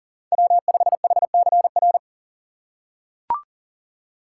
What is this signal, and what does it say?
Morse code, heard as a keyed tone, about 40 wpm, 700 Hz. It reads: W5HCR
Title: USA Call Signs - 5 Characters - 2x Character Spacing - 40wpm